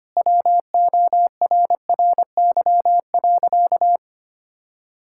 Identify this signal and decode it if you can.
Morse code, heard as a keyed tone, about 25 wpm, 700 Hz. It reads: WORRY.
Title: Pangram Challenge v2 - 25wpm